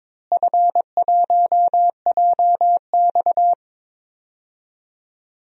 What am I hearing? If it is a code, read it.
Morse code, heard as a keyed tone, about 22 wpm, 700 Hz. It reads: F1JX
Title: DXCC-100 Call Signs - 4 Characters - Repeated 3x - 22wpm